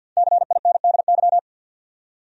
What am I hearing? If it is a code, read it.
Morse code, heard as a keyed tone, about 53 wpm, 700 Hz. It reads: CINDY
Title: USA Common Names - Max 5 Letters - 50wpm